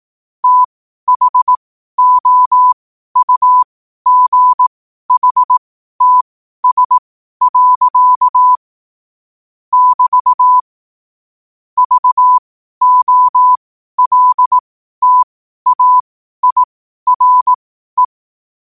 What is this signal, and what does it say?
Morse code, heard as a keyed tone, about 18 wpm, spaced as for 12 wpm, 1000 Hz. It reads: THOUGHTS. <BT> VOLTAIRE
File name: qotdfiles / 12wpm